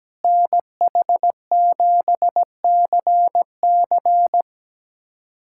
Morse code, heard as a keyed tone, about 17 wpm, 700 Hz. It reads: NH7CC
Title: USA Callsign DE Callsign - 5 Characters - 17wpm